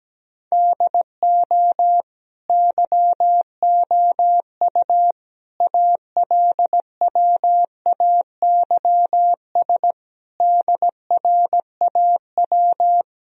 Morse code, heard as a keyed tone, about 17 wpm, 700 Hz. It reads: DO YOU ALWAYS DRAW
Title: Sentences from Top 1000 Words - Review Part 1 - 17wpm